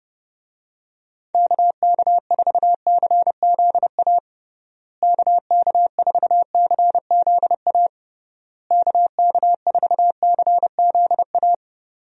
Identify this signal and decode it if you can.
Morse code, heard as a keyed tone, about 30 wpm, 700 Hz. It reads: KK4CZA KK4CZA KK4CZA
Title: USA Call Signs - 6 Characters - Repeated 3x - 30wpm